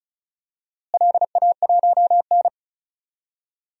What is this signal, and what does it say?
Morse code, heard as a keyed tone, about 34 wpm, 700 Hz. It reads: LA1N
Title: DXCC-100 Call Signs - 4 Characters - Repeated 2x - 35wpm